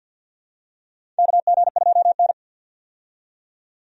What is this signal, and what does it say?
Morse code, heard as a keyed tone, about 48 wpm, 700 Hz. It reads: KG2N
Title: USA Call Signs - 4 Characters - 50wpm